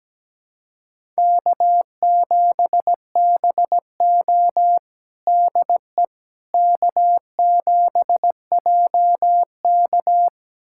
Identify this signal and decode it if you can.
Morse code, heard as a keyed tone, about 17 wpm, 700 Hz. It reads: K7BO DE K7JK
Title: USA Callsign DE Callsign - 4 Characters - 17wpm